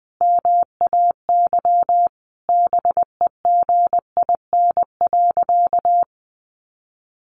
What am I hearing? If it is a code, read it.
Morse code, heard as a keyed tone, about 20 wpm, 700 Hz. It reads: MAY BEGIN.
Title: Sentences from Top 300 Words 20wpm